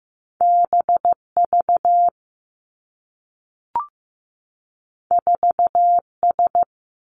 Morse code, heard as a keyed tone, about 15 wpm, 700 Hz. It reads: B V 4 S
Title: Two Challenging Letter-Number Combinations 15wpm